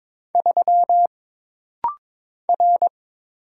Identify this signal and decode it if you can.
Morse code, heard as a keyed tone, about 22 wpm, 700 Hz. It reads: 3 R
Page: This rapid-fire timing is geared to push your ICR (Instant Character Recognition) skills.